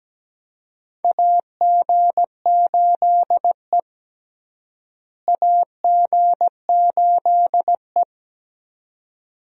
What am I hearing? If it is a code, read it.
Morse code, heard as a keyed tone, about 17 wpm, 700 Hz. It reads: AG8E AG8E
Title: USA Call Signs - 4 Characters - Repeated 2x - 17wpm